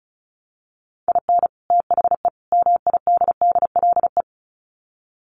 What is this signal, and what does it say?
Morse code, heard as a keyed tone, about 34 wpm, 700 Hz. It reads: IN THE MIDDLE
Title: Sets of 3 Words 35wpm